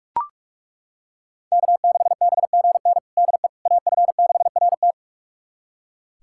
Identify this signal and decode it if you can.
Morse code, heard as a keyed tone, about 42 wpm, 700 Hz. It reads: K6BGN DE AF6RT
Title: USA Callsign DE Callsign - 5 Characters - 45wpm